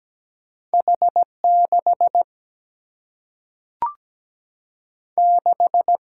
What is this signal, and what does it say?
Morse code, heard as a keyed tone, about 17 wpm, 700 Hz. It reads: H 6 6
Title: Single-Letter Single-Number 17wpm